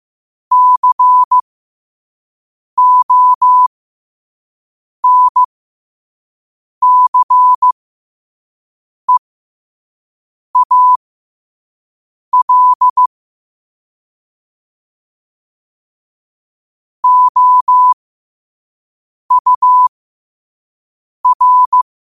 Morse code, heard as a keyed tone, about 15 wpm, spaced as for 5 wpm, 1000 Hz. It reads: CONCEAL OUR
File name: qotdfiles / 5wpm